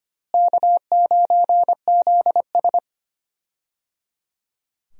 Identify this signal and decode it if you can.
Morse code, heard as a keyed tone, about 25 wpm, 700 Hz. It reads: K9ZS
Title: USA Call Signs - 4 Characters - Repeated 3x - 25wpm